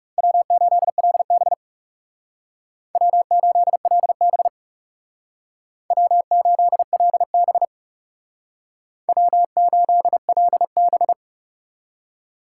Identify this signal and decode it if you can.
Morse code, heard as a keyed tone, about 39 wpm, 700 Hz. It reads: W8LB W8LB W8LB W8LB